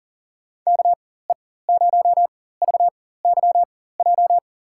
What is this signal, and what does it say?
Morse code, heard as a keyed tone, about 40 wpm, spaced as for 19 wpm, 700 Hz. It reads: KE0VYJ
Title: USA Call Signs - 6 Characters - 4x Character Spacing - 40wpm